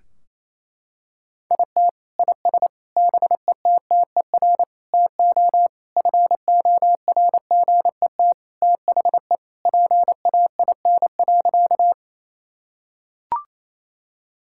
Morse code, heard as a keyed tone, about 28 wpm, 700 Hz. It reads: IT IS BETTER TO FORGET THE PAIN.
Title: Sentences from Top 1600 Words 28wpm